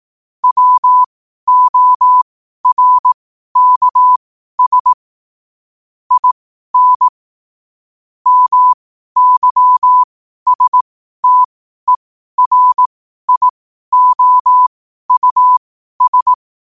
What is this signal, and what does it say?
Morse code, heard as a keyed tone, about 18 wpm, spaced as for 12 wpm, 1000 Hz. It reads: WORKS IN MYSTERIOUS
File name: qotdfiles / 12wpm